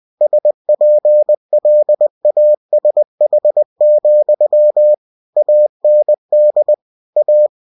Morse code, heard as a keyed tone, about 20 wpm, 600 Hz. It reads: SPLASH, AND A
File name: cw-books / winnie-the-pooh / winnie-20wpm-page-0085